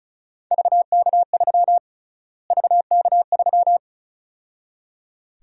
Morse code, heard as a keyed tone, about 36 wpm, 700 Hz. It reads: VK3 VK3
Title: USA Call Sign Prefixes - Repeated 2x - 35wpm